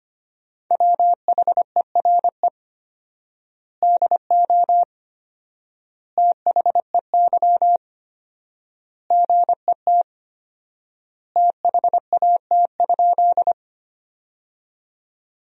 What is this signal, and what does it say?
Morse code, heard as a keyed tone, about 25 wpm, 700 Hz. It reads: WHERE DO THEY GET THAT?
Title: 4x Word-Spacing - Sentences from Top 200 Words - 25wpm